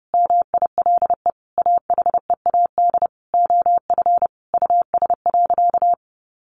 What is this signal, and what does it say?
Morse code, heard as a keyed tone, about 30 wpm, 700 Hz. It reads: MILE AHEAD OF US.
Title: Sentences from Top 1300 Words 30wpm